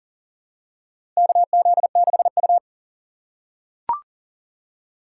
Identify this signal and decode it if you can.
Morse code, heard as a keyed tone, about 40 wpm, 700 Hz. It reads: KZ6U
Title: USA Call Signs - 4 Characters - 3x Character Spacing - 40wpm